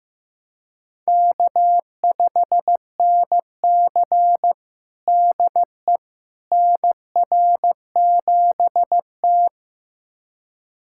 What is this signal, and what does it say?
Morse code, heard as a keyed tone, about 15 wpm, 700 Hz. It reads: K5NC DE NR7T
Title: USA Callsign DE Callsign - 4 Characters - 15wpm